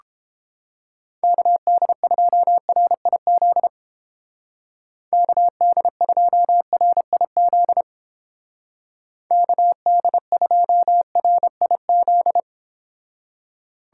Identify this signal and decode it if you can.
Morse code, heard as a keyed tone, about 29 wpm, 700 Hz. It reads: KD2RIZ KD2RIZ KD2RIZ